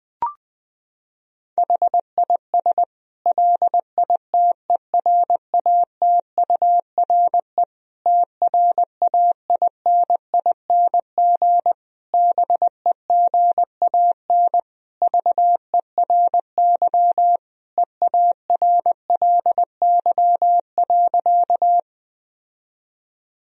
Morse code, heard as a keyed tone, about 21 wpm, 700 Hz. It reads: HIS LITERATURE TRAINING BEGAN VERY EARLY.
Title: Sentences from Top 1000 Words - Review Part 4 - 20wpm